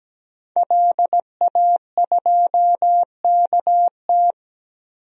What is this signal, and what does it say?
Morse code, heard as a keyed tone, about 17 wpm, 700 Hz. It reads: LA2KT
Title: DXCC-100 Call Signs - 5 Characters - 17wpm